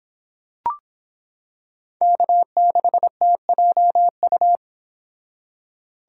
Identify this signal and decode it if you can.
Morse code, heard as a keyed tone, about 26 wpm, 700 Hz. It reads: K6TJU